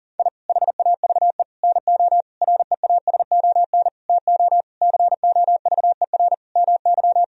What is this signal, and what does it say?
Morse code, heard as a keyed tone, about 40 wpm, 700 Hz. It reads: I HAVE NO REASON TO COVER MY
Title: Sentences from Top 1000 Words - Review Part 1 - 40wpm